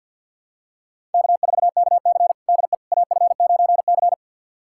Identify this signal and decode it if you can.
Morse code, heard as a keyed tone, about 50 wpm, 700 Hz. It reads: K4KC DE AF9C
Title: USA Callsign DE Callsign - 4 Characters - 50wpm